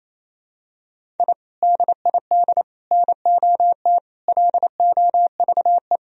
Morse code, heard as a keyed tone, about 28 wpm, 700 Hz. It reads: I DID NOT LOVE
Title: Sentences from Top 1000 Words - Review Part 2 - 28wpm